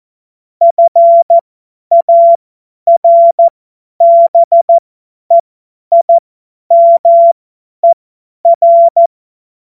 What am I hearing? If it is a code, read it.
Morse code, heard as a keyed tone, about 14 wpm, spaced as for 10 wpm, 680 Hz. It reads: FARBEIMER
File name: Morsekurs / Witze / Witz 24.Oktober 2023 50 F70